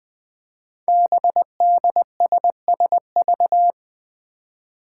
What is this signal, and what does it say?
Morse code, heard as a keyed tone, about 20 wpm, 700 Hz. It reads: BDSSV
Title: Five Challenging Letter-Number Combinations 20wpm